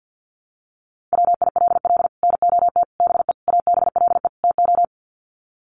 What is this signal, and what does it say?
Morse code, heard as a keyed tone, about 50 wpm, 700 Hz. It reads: WILL NOT BE ABLE TO
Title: Sets of 5 Words 50wpm